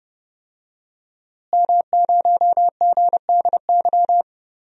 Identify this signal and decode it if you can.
Morse code, heard as a keyed tone, about 30 wpm, 700 Hz. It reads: M0GDY
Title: DXCC-100 Call Signs - 5 Characters - Repeated 2x - 30wpm